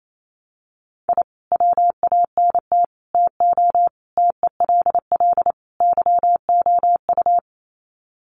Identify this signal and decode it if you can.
Morse code, heard as a keyed tone, about 28 wpm, 700 Hz. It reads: I WANT TO TELL YOU
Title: Sets of 5 Words Encore 28wpm